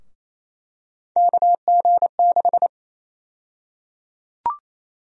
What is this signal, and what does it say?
Morse code, heard as a keyed tone, about 28 wpm, 700 Hz. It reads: KG6
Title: USA Call Sign Prefixes - Repeated 2x - 28wpm